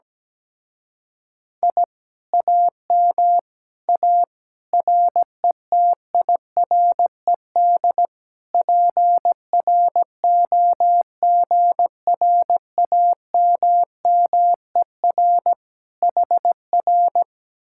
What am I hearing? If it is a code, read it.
Morse code, heard as a keyed tone, about 17 wpm, 700 Hz. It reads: I AM A RETIRED PROGRAMMER HR
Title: Speed-Racing Common QSO Phrases - 17wpm